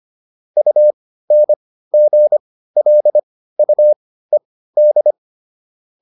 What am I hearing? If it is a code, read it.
Morse code, heard as a keyed tone, about 25 wpm, spaced as for 15 wpm, 600 Hz. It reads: UNGLUED